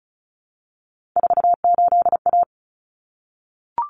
Morse code, heard as a keyed tone, about 34 wpm, 700 Hz. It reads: 48A
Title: Common Cut Numbers - 3-Digit 35wpm